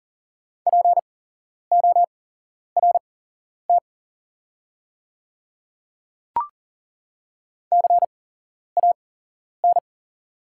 Morse code, heard as a keyed tone, about 40 wpm, spaced as for 11 wpm, 700 Hz. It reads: PORT CAN
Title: 8x Character Spacing - Top 300 Words - 40wpm